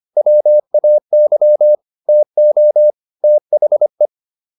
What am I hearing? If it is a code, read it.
Morse code, heard as a keyed tone, about 25 wpm, 600 Hz. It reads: WAY TO THE